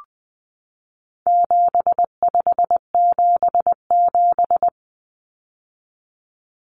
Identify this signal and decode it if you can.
Morse code, heard as a keyed tone, about 20 wpm, 700 Hz. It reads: 7577
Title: Common Cut Numbers - 4-Digit 20wpm